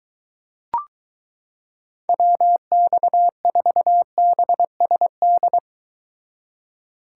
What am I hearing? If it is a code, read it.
Morse code, heard as a keyed tone, about 23 wpm, 700 Hz. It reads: WX4BSD